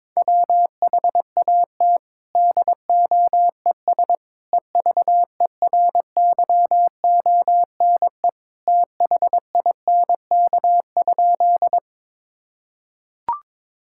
Morse code, heard as a keyed tone, about 22 wpm, 700 Hz. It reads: WHAT DOES EVERYONE THINK?
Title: Sentences from Top 1000 Words - Review Part 5 - 22wpm